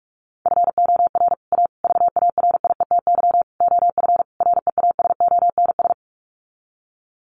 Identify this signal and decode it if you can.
Morse code, heard as a keyed tone, about 44 wpm, 700 Hz. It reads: FOR A VARIETY OF REASONS
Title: Sets of 5 Words Encore 45wpm